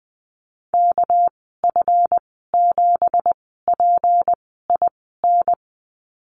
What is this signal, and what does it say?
Morse code, heard as a keyed tone, about 20 wpm, spaced as for 15 wpm, 700 Hz. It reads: KF7PIN
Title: 213 Lesson 34g - Number 8 - Call Signs 20wpm